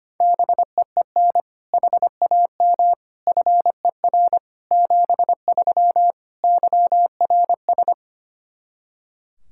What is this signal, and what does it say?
Morse code, heard as a keyed tone, about 25 wpm, 700 Hz. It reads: BEEN HAM FER 73 YRS
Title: Practice QSO - Exchange 3 - More Info - 25wpm